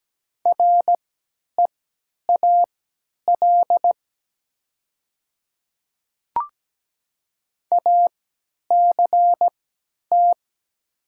Morse code, heard as a keyed tone, about 17 wpm, spaced as for 10 wpm, 700 Hz. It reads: REAL ACT
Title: 3x Character Spacing - Top 300 Words - 17wpm